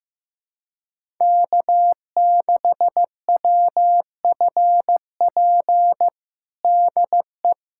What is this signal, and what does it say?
Morse code, heard as a keyed tone, about 16 wpm, 700 Hz. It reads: K6WFP DE
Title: USA Callsign DE Callsign 15wpm